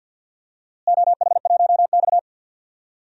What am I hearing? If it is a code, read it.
Morse code, heard as a keyed tone, about 52 wpm, 700 Hz. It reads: OH1X
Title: DXCC-100 Call Signs - 4 Characters - Repeated 3x - 50wpm